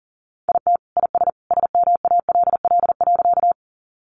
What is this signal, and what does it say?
Morse code, heard as a keyed tone, about 40 wpm, 700 Hz. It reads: IT IS SMALL.
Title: Additional Easy Phrases 40wpm